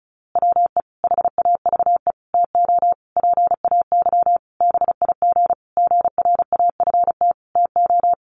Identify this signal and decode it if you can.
Morse code, heard as a keyed tone, about 34 wpm, 700 Hz. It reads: WE HAVE TO PAY BIG GRAFT TO
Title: A Sound Of Thunder 35wpm